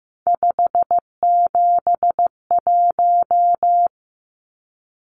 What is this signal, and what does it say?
Morse code, heard as a keyed tone, about 15 wpm, 700 Hz. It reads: 571
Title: Numbers - 3-Digit 15wpm